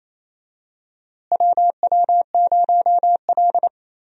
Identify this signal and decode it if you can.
Morse code, heard as a keyed tone, about 28 wpm, 700 Hz. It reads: WW0L
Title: USA Call Signs - 4 Characters - Repeated 2x - 28wpm